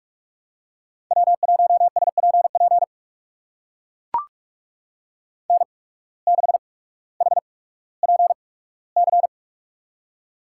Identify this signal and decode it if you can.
Morse code, heard as a keyed tone, about 43 wpm, spaced as for 19 wpm, 700 Hz. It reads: W1SPP N6HPC
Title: USA Call Signs - 5 Characters - 8x Character Spacing - 45wpm